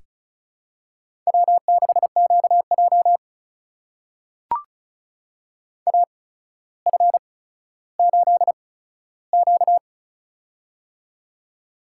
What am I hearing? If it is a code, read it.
Morse code, heard as a keyed tone, about 36 wpm, spaced as for 15 wpm, 700 Hz. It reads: W6QJ AF8Q
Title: USA Call Signs - 4 Characters - 8x Character Spacing - 35wpm